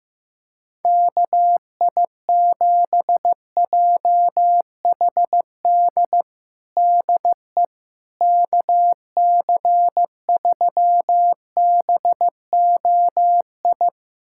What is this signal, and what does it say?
Morse code, heard as a keyed tone, about 15 wpm, 700 Hz. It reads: KI7JHD DE KC3BOI
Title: USA Callsign DE Callsign 15wpm